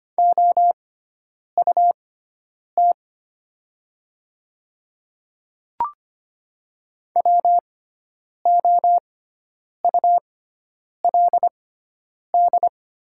Morse code, heard as a keyed tone, about 25 wpm, spaced as for 9 wpm, 700 Hz. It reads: OUT WOULD
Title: 6x Character Spacing - Top 100 Words - 25wpm